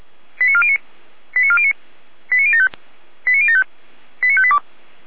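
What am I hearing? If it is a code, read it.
Five-tone selective call (ZVEI), heard as quick runs of five tones, tones of about 70 ms. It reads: 89409 89409 89075 89075 89582
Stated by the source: ZVEI five-tone call 89018 received by SDR.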